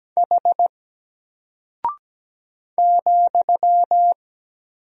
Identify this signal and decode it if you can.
Morse code, heard as a keyed tone, about 17 wpm, 700 Hz. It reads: H ,
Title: Single Character (Letter:Number:Punctuation:Prosign) Alt 17wpm